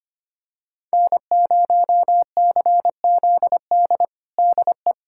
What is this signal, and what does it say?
Morse code, heard as a keyed tone, about 25 wpm, 700 Hz. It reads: N0CZD DE
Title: USA Callsign DE Callsign - 5 Characters - 25wpm